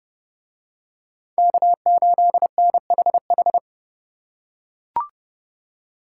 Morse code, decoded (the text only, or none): K8NHH